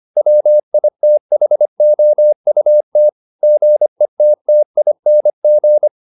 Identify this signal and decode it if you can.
Morse code, heard as a keyed tone, about 25 wpm, 600 Hz. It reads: WITHOUT GETTING
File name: the old man and the sea part 1 25wpm